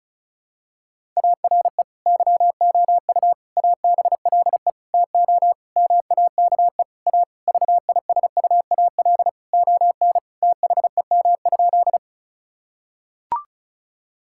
Morse code, heard as a keyed tone, about 35 wpm, 700 Hz. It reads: ARE YOU ABLE TO MAKE A VISUAL ON THEM?
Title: Sentences from Top 1000 Words - Review Part 6 - 35wpm